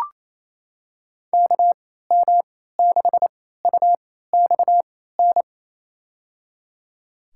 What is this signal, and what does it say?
Morse code, heard as a keyed tone, about 28 wpm, spaced as for 16 wpm, 700 Hz. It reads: KM6UXN
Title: USA Call Signs - 6 Characters - 3x Character Spacing - 28wpm